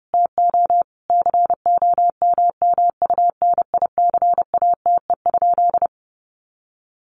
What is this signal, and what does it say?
Morse code, heard as a keyed tone, about 30 wpm, 700 Hz. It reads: TO COMMUNICATE?
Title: Sentences from Top 1300 Words 30wpm